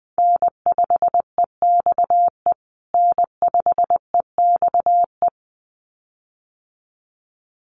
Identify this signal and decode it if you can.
Morse code, heard as a keyed tone, about 20 wpm, 700 Hz. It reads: N5EXE N5EXE
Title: USA Call Signs Repeated 2x 20wpm